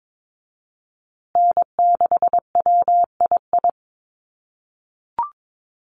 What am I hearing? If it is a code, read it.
Morse code, heard as a keyed tone, about 22 wpm, 700 Hz. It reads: N6WII
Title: USA Call Signs Repeated 3x 22wpm